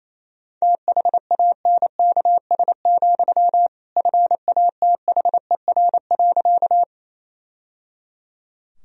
Morse code, heard as a keyed tone, about 28 wpm, 700 Hz. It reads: THANKS, FATHER.
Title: Sentences from Top 1000 Words - Review Part 3 - 28wpm